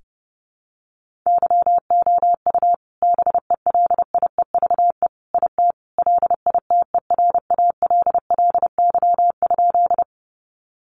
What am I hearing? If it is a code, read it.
Morse code, heard as a keyed tone, about 30 wpm, 700 Hz. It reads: YOU BELIEVE IT LITERALLY?
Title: Sentences from Top 1400 Words 30wpm